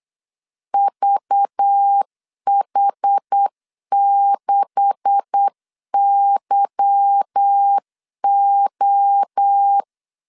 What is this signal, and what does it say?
Morse code, heard as a keyed tone, about 8 wpm, 800 Hz. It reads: VH6YO
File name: LEZIONI / 9 286 8wpm 3425